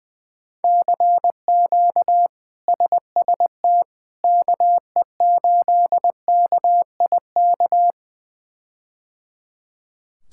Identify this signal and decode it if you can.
Morse code, heard as a keyed tone, about 20 wpm, 700 Hz. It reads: CQ SST KE8KIK
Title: K1USN SST QSO Practice - 20wpm